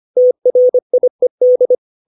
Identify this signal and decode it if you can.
Morse code, heard as a keyed tone, about 25 wpm, 500 Hz. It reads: TRIED
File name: 632 words / 632 Words 25WPM